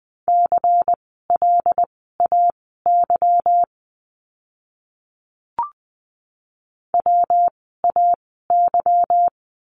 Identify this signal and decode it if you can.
Morse code, heard as a keyed tone, about 20 wpm, spaced as for 15 wpm, 700 Hz. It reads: CLAY WAY
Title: 146 Lesson 24f - Letter y - Words 20wpm